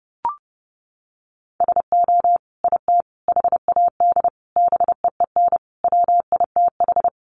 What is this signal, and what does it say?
Morse code, heard as a keyed tone, about 30 wpm, 700 Hz. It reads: SO IT HAD BEEN WITH
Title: Sentences from Top 1200 Words 30wpm